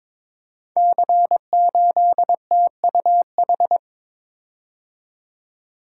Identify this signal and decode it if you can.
Morse code, heard as a keyed tone, about 22 wpm, 700 Hz. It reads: C8TUH
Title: Code Groups - 5 Letter:Numbers - 22wpm